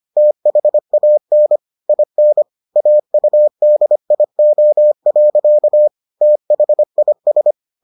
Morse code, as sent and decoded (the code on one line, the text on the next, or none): - .... .- -. / .. -. / .- ..- -.. .. --- .-.-.- / - .... .. ...
THAN IN AUDIO. THIS